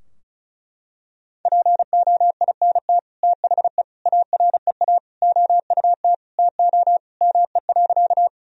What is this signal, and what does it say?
Morse code, heard as a keyed tone, about 36 wpm, 700 Hz. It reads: POINT THE AREA OUT TO ME.
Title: Sentences from Top 1000 Words - Review Part 2 - 35wpm